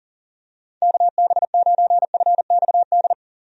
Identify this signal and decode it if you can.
Morse code, heard as a keyed tone, about 40 wpm, 700 Hz. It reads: KB9FXD
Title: USA Call Signs - 6 Characters - 40wpm